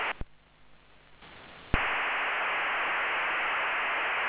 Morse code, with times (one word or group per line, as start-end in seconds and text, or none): none